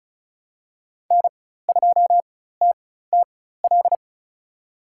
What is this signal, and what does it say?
Morse code, heard as a keyed tone, about 34 wpm, spaced as for 16 wpm, 700 Hz. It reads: N2TTL
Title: USA Call Signs - 5 Characters - 4x Character Spacing - 35wpm